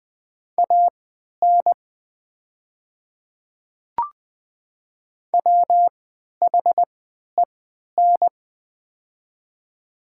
Morse code, heard as a keyed tone, about 20 wpm, spaced as for 11 wpm, 700 Hz. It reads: AN WHEN
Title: 3x Character Spacing - Top 300 Words - 20wpm